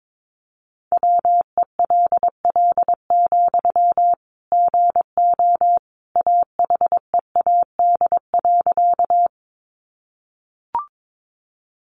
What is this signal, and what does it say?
Morse code, heard as a keyed tone, about 22 wpm, 700 Hz. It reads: WELL, GO AHEAD.
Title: Sentences from Top 1100 Words 22wpm